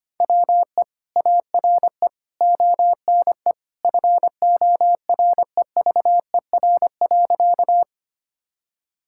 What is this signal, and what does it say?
Morse code, heard as a keyed tone, about 25 wpm, 700 Hz. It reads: WE ARE ONE FOREVER.
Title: Sentences from Top 1600 Words 25wpm